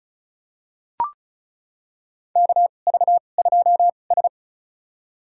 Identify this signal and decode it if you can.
Morse code, heard as a keyed tone, about 36 wpm, spaced as for 25 wpm, 700 Hz. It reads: KV2S
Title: USA Call Signs - 4 Characters - 2x Character Spacing - 35wpm